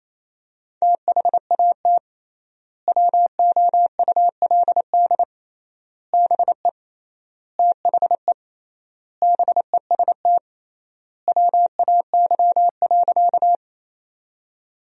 Morse code, heard as a keyed tone, about 28 wpm, 700 Hz. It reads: THAT WOULD BE THE BEST WAY.
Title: 3x Word-Spacing - Sentences from Top 400 Words - 28wpm